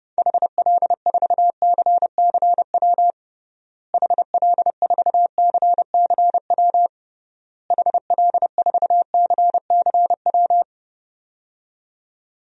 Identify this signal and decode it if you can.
Morse code, heard as a keyed tone, about 30 wpm, 700 Hz. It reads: HL4CCW HL4CCW HL4CCW
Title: DXCC-100 Call Signs - 6 Characters - Repeated 3x - 30wpm